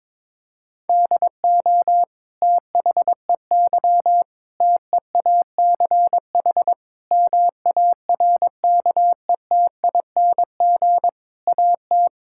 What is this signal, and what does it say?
Morse code, heard as a keyed tone, about 22 wpm, 700 Hz. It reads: DO THEY TEACH MARKETING AT
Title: Sentences from Top 1000 Words - Review Part 5 - 22wpm